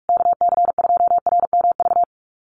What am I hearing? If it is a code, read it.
Morse code, heard as a keyed tone, about 44 wpm, 700 Hz. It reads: KC2RMV
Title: USA Call Signs Repeated 3x 45wpm